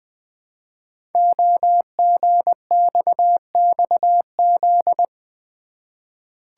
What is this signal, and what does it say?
Morse code, heard as a keyed tone, about 20 wpm, 700 Hz. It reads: OGXXZ
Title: Code Groups - 5 Letter:Numbers - 20wpm